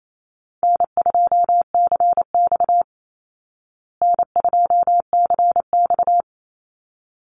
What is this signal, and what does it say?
Morse code, heard as a keyed tone, about 28 wpm, 700 Hz. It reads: N2CX N2CX
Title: Easy Call Signs Repeated 2x - 28wpm